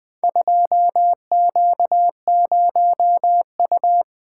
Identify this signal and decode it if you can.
Morse code, heard as a keyed tone, about 20 wpm, 700 Hz. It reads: 2Q0U
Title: Four Letter-Number Combinations - 20wpm